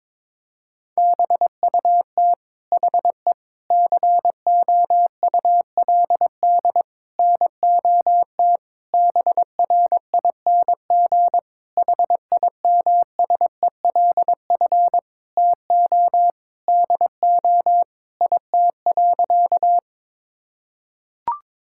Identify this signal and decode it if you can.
Morse code, heard as a keyed tone, about 22 wpm, 700 Hz. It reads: BUT HE COULD NOT BRING HIMSELF TO DO IT.
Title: Sentences from Top 1600 Words 22wpm